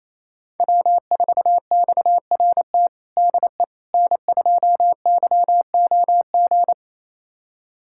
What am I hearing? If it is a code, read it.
Morse code, heard as a keyed tone, about 28 wpm, 700 Hz. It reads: W4XRT DE N2YOG
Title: USA Callsign DE Callsign 28wpm